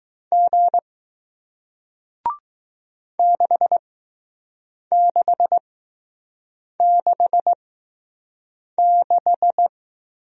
Morse code, heard as a keyed tone, about 20 wpm, spaced as for 6 wpm, 700 Hz. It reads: G 6666